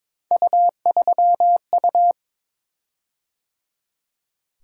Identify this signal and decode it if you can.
Morse code, heard as a keyed tone, about 22 wpm, 700 Hz. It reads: U3U
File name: practice / 22wpm / Three Letter-Number Combinations - 22wpm